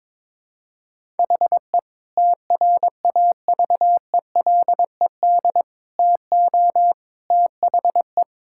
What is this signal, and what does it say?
Morse code, heard as a keyed tone, about 22 wpm, 700 Hz. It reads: HE TRAVELED TO THE
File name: practice / 22wpm / Sentences with 3-Word Phrases - 22wpm